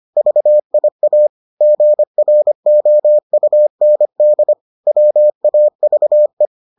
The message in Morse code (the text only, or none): VIA GROUND WAVE